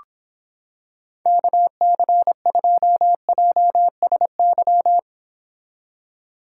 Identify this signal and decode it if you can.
Morse code, heard as a keyed tone, about 26 wpm, 700 Hz. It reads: KC2JSY